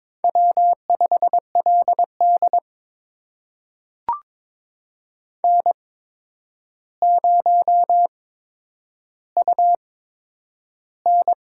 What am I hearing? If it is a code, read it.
Morse code, heard as a keyed tone, about 22 wpm, spaced as for 9 wpm, 700 Hz. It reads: W5LD N0UN
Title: USA Call Signs - 4 Characters - 8x Character Spacing - 22wpm